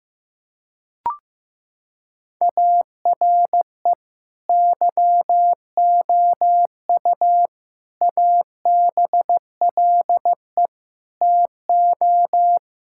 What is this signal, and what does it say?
Morse code, heard as a keyed tone, about 15 wpm, 700 Hz. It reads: ARE YOU ABLE TO
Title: Sentences from Top 1000 Words - Review Part 5 - 15wpm